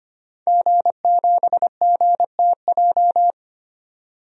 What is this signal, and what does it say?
Morse code, heard as a keyed tone, about 25 wpm, 700 Hz. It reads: G7GTJ
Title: DXCC-100 Call Signs - 5 Characters - 25wpm